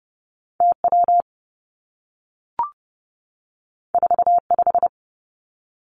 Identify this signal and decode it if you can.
Morse code, heard as a keyed tone, about 30 wpm, 700 Hz. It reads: TW 45
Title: Two Challenging Letter-Number Combinations 30wpm